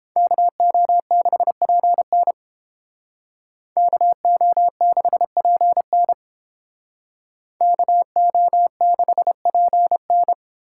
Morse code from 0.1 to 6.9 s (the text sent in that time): KO6PN KO6PN